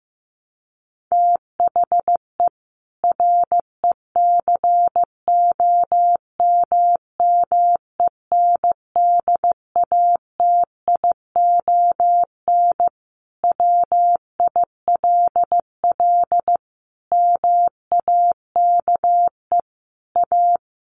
Morse code, heard as a keyed tone, about 15 wpm, 700 Hz. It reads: THE RECOMMENDATION WILL MAKE A
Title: Sentences from Top 1400 Words 15wpm